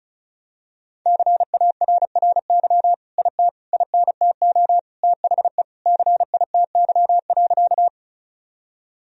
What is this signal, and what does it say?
Morse code, heard as a keyed tone, about 36 wpm, 700 Hz. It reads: CARRY IT INTO THE CITY.
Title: Sentences from Top 1000 Words - Review Part 1 - 35wpm